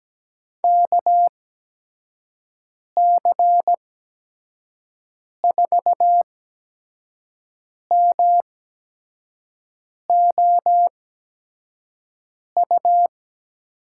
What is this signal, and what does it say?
Morse code, heard as a keyed tone, about 17 wpm, spaced as for 5 wpm, 700 Hz. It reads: KC4MOU
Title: USA Call Signs - 6 Characters - 8x Character Spacing - 17wpm